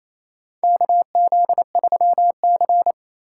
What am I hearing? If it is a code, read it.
Morse code, heard as a keyed tone, about 28 wpm, 700 Hz. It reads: KZ3C
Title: USA Call Signs - 4 Characters - 3x Character Spacing - 28wpm